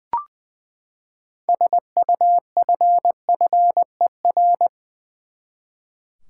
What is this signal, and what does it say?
Morse code, heard as a keyed tone, about 20 wpm, 700 Hz. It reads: SUFFER
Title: Six Letter Words - 20wpm